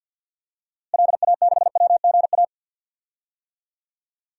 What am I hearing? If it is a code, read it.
Morse code, heard as a keyed tone, about 51 wpm, 700 Hz. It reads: LA6WGA
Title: DXCC-100 Call Signs - 6 Characters - Repeated 2x - 50wpm